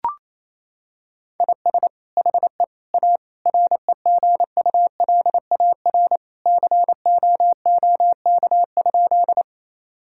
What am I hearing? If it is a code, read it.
Morse code, heard as a keyed tone, about 28 wpm, 700 Hz. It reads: IS HE A REGULAR COOK?
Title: Sentences from Top 1000 Words - Review Part 6 - 28wpm